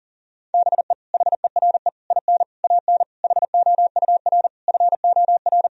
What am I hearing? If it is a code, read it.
Morse code, heard as a keyed tone, about 37 wpm, 700 Hz. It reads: BE HERE IN AN HOUR FOR
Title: Sentences from Top 1000 Words - Review Part 4 - 40wpm